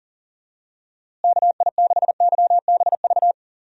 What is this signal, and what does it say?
Morse code, heard as a keyed tone, about 37 wpm, 700 Hz. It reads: KI6YBV
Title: USA Call Signs - 6 Characters - Repeated 3x - 40wpm